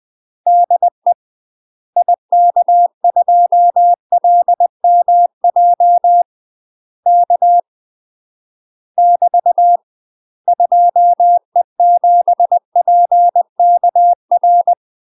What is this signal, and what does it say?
Morse code, heard as a keyed tone, about 20 wpm, 700 Hz. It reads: DE IK2LMJ K <BT> 2E7PKR
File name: UK MASTER QSOs TRAINING / 20WPM / UK MASTER QSOs 0068